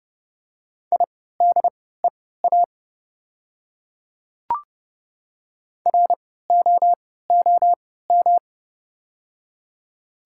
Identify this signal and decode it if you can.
Morse code, heard as a keyed tone, about 30 wpm, spaced as for 17 wpm, 700 Hz. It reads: IDEA ROOM
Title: 3x Character Spacing - Top 300 Words - 30wpm